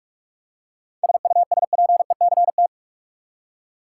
Morse code, heard as a keyed tone, about 46 wpm, 700 Hz. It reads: SUSPECT
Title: Seven Letter Words - 45wpm